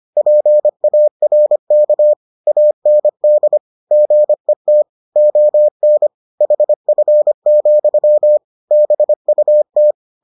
Morse code, heard as a keyed tone, about 25 wpm, 600 Hz. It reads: PARK AND GET ON HF, BUT